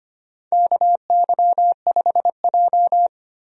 Morse code, heard as a keyed tone, about 25 wpm, 700 Hz. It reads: KY5J
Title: USA Call Signs - 4 Characters - 25wpm